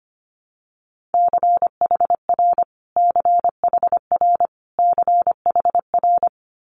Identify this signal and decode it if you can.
Morse code, heard as a keyed tone, about 25 wpm, 700 Hz. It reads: CHR CHR CHR
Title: Multi-letter Phonemes Repeated 3x 25wpm